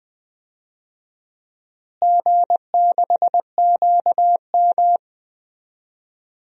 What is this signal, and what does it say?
Morse code, heard as a keyed tone, about 21 wpm, 700 Hz. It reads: G6QM
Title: DXCC-100 Call Signs - 4 Characters - Repeated 2x - 20wpm